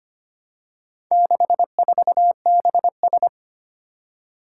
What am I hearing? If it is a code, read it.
Morse code, heard as a keyed tone, about 25 wpm, 700 Hz. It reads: 64BS
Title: Four Challenging Letter-Number Combinations 25wpm